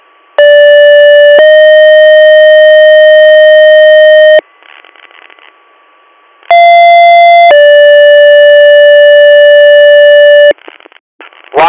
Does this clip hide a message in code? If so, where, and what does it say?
Yes: two-tone page from 0.4 to 10.5 s, heard as a one-second tone, then a three-second tone, about 580 then 620 Hz, about 690 then 580 Hz, with speech.